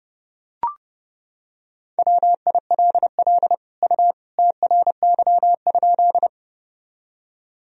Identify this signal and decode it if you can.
Morse code, heard as a keyed tone, about 32 wpm, 700 Hz. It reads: WILL U TRY?
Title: Sentences from Top 200 Words and Abbreviations 30wpm